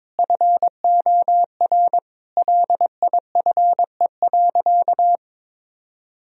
Morse code, heard as a keyed tone, about 22 wpm, 700 Hz. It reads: FOR LIFE.
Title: Sentences from Top 1000 Words - Review Part 2 - 22wpm